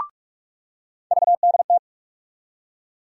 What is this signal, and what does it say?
Morse code, heard as a keyed tone, about 46 wpm, 700 Hz. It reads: VDT
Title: USA Call Sign Postfixes - 3 Letters 45wpm